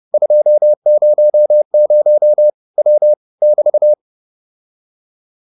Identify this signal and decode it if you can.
Morse code, heard as a keyed tone, about 30 wpm, 600 Hz. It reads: 200 W <BT>